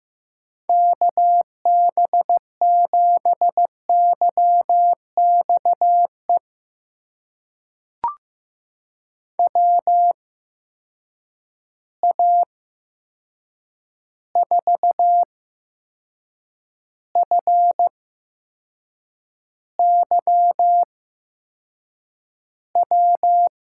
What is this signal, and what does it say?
Morse code, heard as a keyed tone, about 15 wpm, spaced as for 6 wpm, 700 Hz. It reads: KB7YXE WA4FYW
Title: USA Call Signs - 6 Characters - 8x Character Spacing - 15wpm